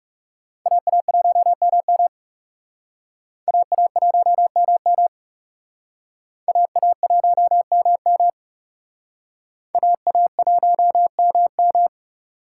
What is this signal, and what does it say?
Morse code, heard as a keyed tone, about 38 wpm, 700 Hz. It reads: AA1MM AA1MM AA1MM AA1MM
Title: Speed Racing - USA 2x2 Call Signs 30wpm